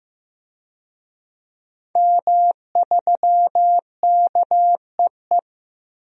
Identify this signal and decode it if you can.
Morse code, heard as a keyed tone, about 16 wpm, 700 Hz. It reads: M3KEE
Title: DXCC-100 Call Signs - 5 Characters - 15wpm